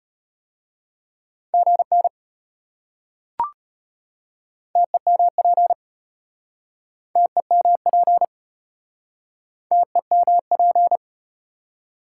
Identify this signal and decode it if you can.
Morse code, heard as a keyed tone, about 34 wpm, 700 Hz. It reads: GN TEMP TEMP TEMP